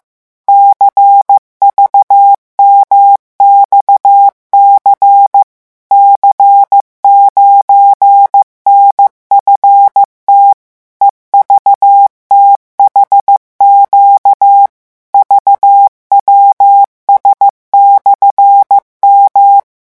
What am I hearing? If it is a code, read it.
Morse code, heard as a keyed tone, about 15 wpm, 800 Hz. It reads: CVMXC C9NFT EVTHQ VWS/M
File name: morse mp3 / revision13-vitesse 14 mots